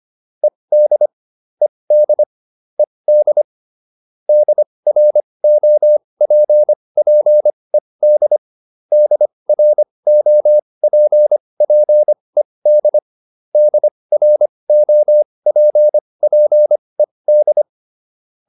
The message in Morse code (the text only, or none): ED ED ED DROPPED DROPPED DROPPED